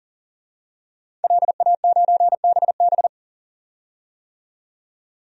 Morse code, heard as a keyed tone, about 40 wpm, 700 Hz. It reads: LA9BB
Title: DXCC-100 Call Signs - 5 Characters - 40wpm